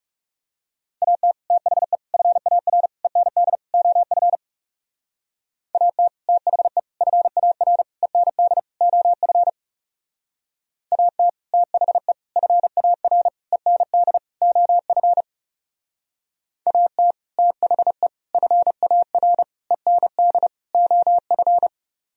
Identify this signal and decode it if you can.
Morse code, heard as a keyed tone, about 34 wpm, 700 Hz. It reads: AT THE FAR END OF AT THE FAR END OF AT THE FAR END OF AT THE FAR END OF
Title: Speed-Racing - Sets of 5 Words 30wpm